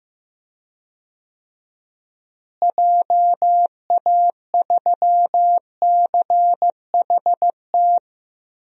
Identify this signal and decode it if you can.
Morse code, heard as a keyed tone, about 15 wpm, 700 Hz. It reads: JA3CHT
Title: DXCC-100 Call Signs - 6 Characters - Repeated 3x - 15wpm